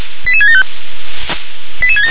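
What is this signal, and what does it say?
Five-tone selective call (ZVEI), heard as quick runs of five tones, tones of about 70 ms. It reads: 88675
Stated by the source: ZVEI five-tone selective call received by SDR.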